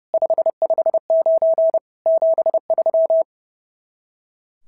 Morse code, heard as a keyed tone, about 28 wpm, 660 Hz. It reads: 559 73
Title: SOTA QSO Practice 30wpm